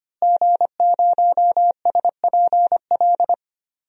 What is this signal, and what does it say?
Morse code, heard as a keyed tone, about 25 wpm, 700 Hz. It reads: G0SPL
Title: DXCC-100 Call Signs - 5 Characters - Repeated 3x - 25wpm